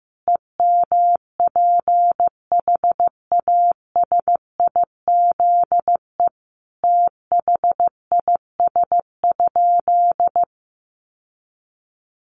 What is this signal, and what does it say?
Morse code, heard as a keyed tone, about 15 wpm, 700 Hz. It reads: EMPHASIZE THIS?
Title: Sentences from Top 1500 Words 15wpm